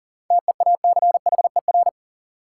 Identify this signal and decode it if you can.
Morse code, heard as a keyed tone, about 40 wpm, 700 Hz. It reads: TEACHER
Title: Sentences from Top 900 Words - New Word Drill - 40wpm